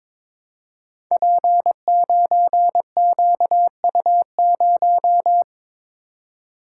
Morse code, heard as a keyed tone, about 22 wpm, 700 Hz. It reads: P9QU0
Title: Code Groups - 5 Letter:Numbers - 22wpm